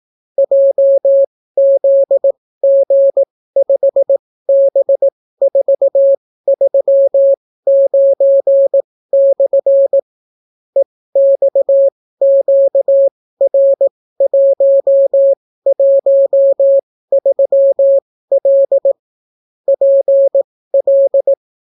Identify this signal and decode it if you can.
Morse code, heard as a keyed tone, about 18 wpm, spaced as for 14 wpm, 550 Hz. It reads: JZG5B439/ EXQR113L PL